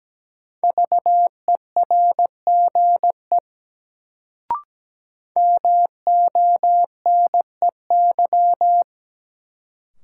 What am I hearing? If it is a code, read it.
Morse code, heard as a keyed tone, about 17 wpm, 700 Hz. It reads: VERGE MONEY
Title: Sets of 4 Words - New Word Drill - 17wpm